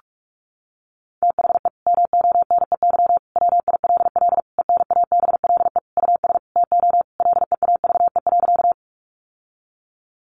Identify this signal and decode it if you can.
Morse code, heard as a keyed tone, about 44 wpm, 700 Hz. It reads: THE MONEY WILL ENABLE US TO LEAVE.
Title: Sentences from Top 1500 Words 45wpm